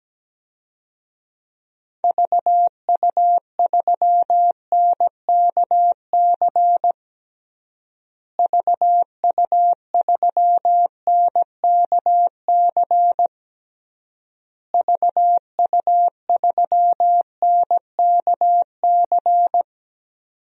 Morse code, heard as a keyed tone, about 17 wpm, 700 Hz. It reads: VU3NKC VU3NKC VU3NKC
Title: DXCC-100 Call Signs - 6 Characters - Repeated 3x - 17wpm